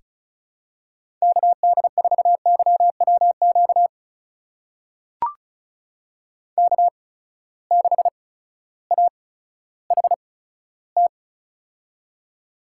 Morse code, heard as a keyed tone, about 34 wpm, spaced as for 16 wpm, 700 Hz. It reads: KD4YWQ K6AHT
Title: USA Call Signs - 8x Character Spacing - 35wpm